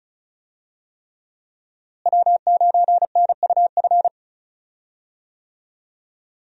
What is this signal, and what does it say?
Morse code, heard as a keyed tone, about 36 wpm, 700 Hz. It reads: W9NUF
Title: USA Call Signs - 5 Characters - Repeated 3x - 35wpm